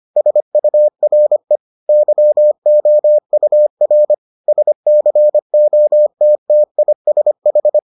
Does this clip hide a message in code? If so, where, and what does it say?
Yes: Morse code from 0.2 to 7.8 s, heard as a keyed tone, about 25 wpm, 600 Hz. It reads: SURE YOUR SCOTTISH